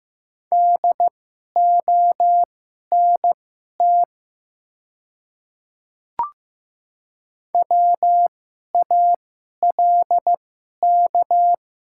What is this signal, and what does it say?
Morse code, heard as a keyed tone, about 15 wpm, spaced as for 11 wpm, 700 Hz. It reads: DONT WALK
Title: 2x Character Spacing - Top 300 Words - 15wpm